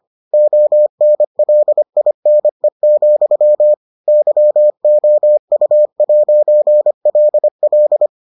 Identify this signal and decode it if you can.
Morse code, heard as a keyed tone, about 25 wpm, 600 Hz. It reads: ONLINE, YOU'LL